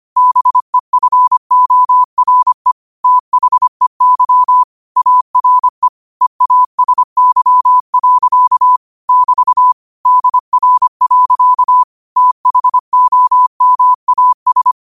Morse code, heard as a keyed tone, about 25 wpm, 1000 Hz. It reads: BEFORE THEY ARE EASY. <BT> DR. THOMAS
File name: qotdfiles / 25wpm